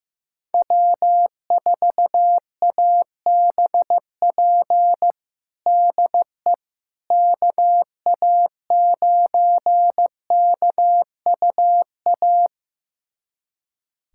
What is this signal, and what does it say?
Morse code, heard as a keyed tone, about 16 wpm, 700 Hz. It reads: W4ABP DE KA9KUA
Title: USA Callsign DE Callsign 15wpm